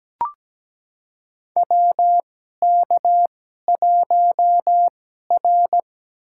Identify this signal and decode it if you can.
Morse code, heard as a keyed tone, about 17 wpm, spaced as for 12 wpm, 700 Hz. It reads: WK1R
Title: USA Call Signs - 2x Character Spacing - 17wpm